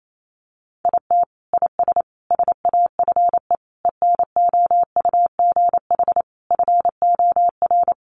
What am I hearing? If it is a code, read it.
Morse code, heard as a keyed tone, about 28 wpm, 700 Hz. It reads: IT IS SAFE ENOUGH FOR
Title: Sentences from Top 900 Words 28wpm